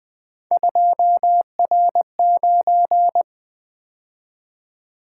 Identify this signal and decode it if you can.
Morse code, heard as a keyed tone, about 20 wpm, 700 Hz. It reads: 2R9
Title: Three Letter-Number Combinations - 20wpm